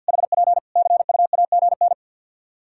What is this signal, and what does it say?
Morse code, heard as a keyed tone, about 50 wpm, 700 Hz. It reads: HP CUAGN
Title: Practice QSO - Exchange 4 - Ending - 50wpm